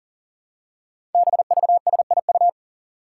Morse code, heard as a keyed tone, about 40 wpm, 700 Hz. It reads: BVSIU
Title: Five Challenging Letter-Number Combinations 40wpm